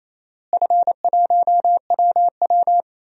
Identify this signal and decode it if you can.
Morse code, heard as a keyed tone, about 28 wpm, 700 Hz. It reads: F1WW
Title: DXCC-100 Call Signs - 4 Characters - Repeated 2x - 28wpm